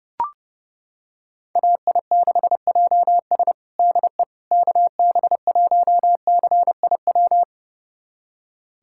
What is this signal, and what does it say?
Morse code, heard as a keyed tone, about 32 wpm, 700 Hz. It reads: AI6JS DE KB1CIW
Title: USA Callsign DE Callsign 30wpm